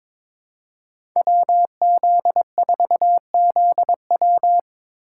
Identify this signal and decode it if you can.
Morse code, heard as a keyed tone, about 22 wpm, 700 Hz. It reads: WZ4ZW
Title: USA Call Signs - 5 Characters - 3x Character Spacing - 22wpm